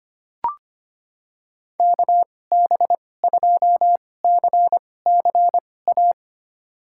Morse code, heard as a keyed tone, about 25 wpm, spaced as for 18 wpm, 700 Hz. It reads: KB2CCA
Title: USA Call Signs - 6 Characters - 2x Character Spacing - 25wpm